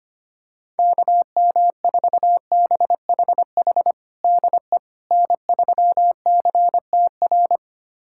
Morse code, heard as a keyed tone, about 25 wpm, 700 Hz. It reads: KM4BHH DE N3CTR
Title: USA Callsign DE Callsign 25wpm